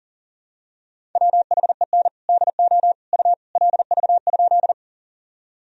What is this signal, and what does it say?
Morse code, heard as a keyed tone, about 40 wpm, 700 Hz. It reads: WHEN DO U LV?
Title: Sentences from Top 400 Words and Abbreviations 40wpm